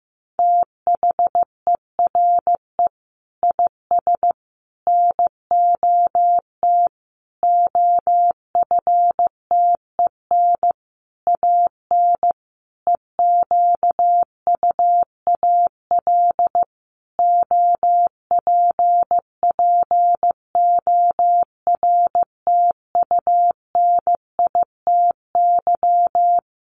Morse code, heard as a keyed tone, about 15 wpm, 700 Hz. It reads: THERE IS NOT OFTEN AN EQUAL OPPORTUNITY
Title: Sentences from Top 1400 Words 15wpm